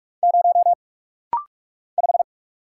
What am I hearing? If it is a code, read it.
Morse code, heard as a keyed tone, about 45 wpm, 700 Hz. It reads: 0 5
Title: Single Letter-Number - Rapid-Fire 45wpm